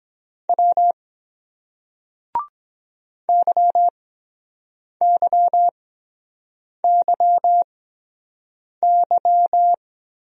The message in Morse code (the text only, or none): W YYYY